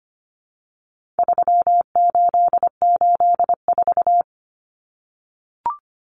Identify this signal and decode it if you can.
Morse code, heard as a keyed tone, about 25 wpm, 700 Hz. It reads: 3884
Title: Common Cut Numbers - 4-Digit 25wpm